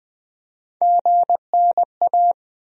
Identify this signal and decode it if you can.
Morse code, heard as a keyed tone, about 20 wpm, 700 Hz. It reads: GNA
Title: USA Call Sign Postfixes - 3 Letters 20wpm